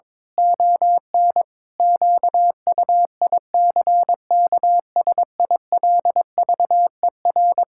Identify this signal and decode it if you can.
Morse code, heard as a keyed tone, about 22 wpm, 700 Hz. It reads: ON QUICKSILVER
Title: Pangram Challenge v2 - 22wpm